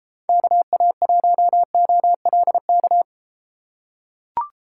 Morse code, heard as a keyed tone, about 34 wpm, 700 Hz. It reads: KA1OLK